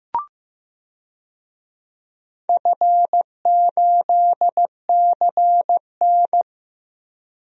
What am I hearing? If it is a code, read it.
Morse code, heard as a keyed tone, about 16 wpm, 700 Hz. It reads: F8CN
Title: DXCC-100 Call Signs - 4 Characters - Repeated 3x - 15wpm